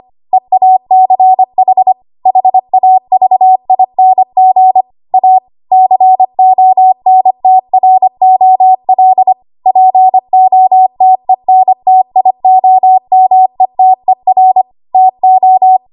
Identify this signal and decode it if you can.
Morse code, heard as a keyed tone, about 25 wpm, 750 Hz. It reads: EACH HAVING A CONTROL POTENTIOMETER TO
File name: Morse / 25WPM